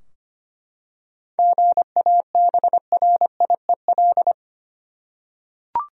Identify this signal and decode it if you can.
Morse code, heard as a keyed tone, about 25 wpm, 700 Hz. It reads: GABRIEL
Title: USA Common Names - 25wpm